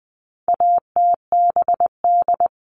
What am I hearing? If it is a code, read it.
Morse code, heard as a keyed tone, about 20 wpm, 700 Hz. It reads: ATBD
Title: All Cut Numbers - 4-Digit 20wpm